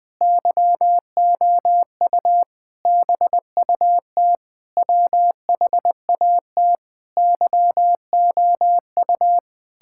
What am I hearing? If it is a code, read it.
Morse code, heard as a keyed tone, about 20 wpm, 700 Hz. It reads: YOU BUT WHAT YOU
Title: Sentences with 9-Word Phrases - Spoken Beginning - 20wpm